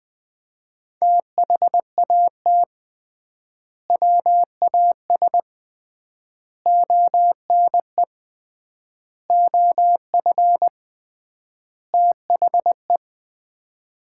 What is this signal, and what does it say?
Morse code, heard as a keyed tone, about 20 wpm, 700 Hz. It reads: THAT WAS ONE OF THE
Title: 3x Word-Spacing - Sets of 5 Words - 20wpm